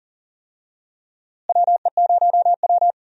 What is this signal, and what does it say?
Morse code, heard as a keyed tone, about 40 wpm, 700 Hz. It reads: WE0W
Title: USA Call Signs - 4 Characters - Repeated 3x - 40wpm